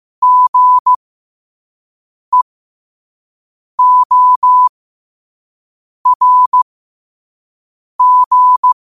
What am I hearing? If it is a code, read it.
Morse code, heard as a keyed tone, about 15 wpm, spaced as for 5 wpm, 1000 Hz. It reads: GEORG
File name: qotdfiles / 5wpm